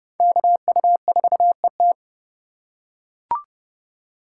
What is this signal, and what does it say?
Morse code, heard as a keyed tone, about 30 wpm, 700 Hz. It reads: KU4ET
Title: USA Call Signs - 4x Character Spacing - 30wpm